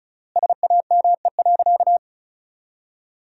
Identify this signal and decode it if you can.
Morse code, heard as a keyed tone, about 36 wpm, 700 Hz. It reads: SAME.
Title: 3x Word-Spacing - Sentences from Top 200 Words - 35wpm